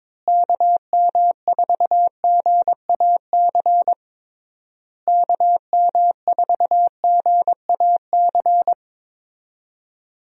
Morse code, heard as a keyed tone, about 22 wpm, 700 Hz. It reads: KM4GAC KM4GAC
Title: USA Call Signs - 6 Characters - Repeated 3x - 22wpm